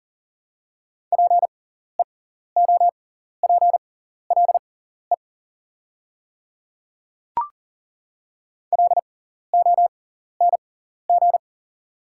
Morse code, heard as a keyed tone, about 41 wpm, spaced as for 14 wpm, 690 Hz. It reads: PEOPLE LONG
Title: 6x Character Spacing - Top 500 Words - 40wpm